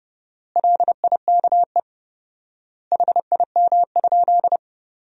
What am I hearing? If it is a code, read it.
Morse code, heard as a keyed tone, about 30 wpm, 700 Hz. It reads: LIKE HIM?
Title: 4x Word-Spacing - Sentences from Top 100 Words - 30wpm